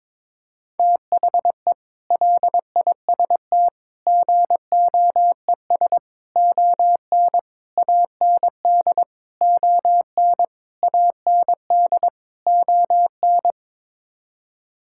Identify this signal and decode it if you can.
Morse code, heard as a keyed tone, about 22 wpm, 700 Hz. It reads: THE LIST GOES ON AND ON AND ON
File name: practice / 22wpm / Sets of 8 Words - 22wpm